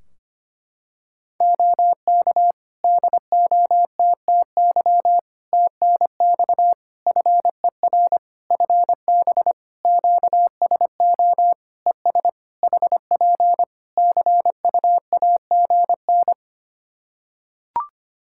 Morse code, heard as a keyed tone, about 25 wpm, 700 Hz. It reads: OK DOTTY TNX FER FB QSO ES HP CUAGN
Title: K1USN SST QSO Practice 25wpm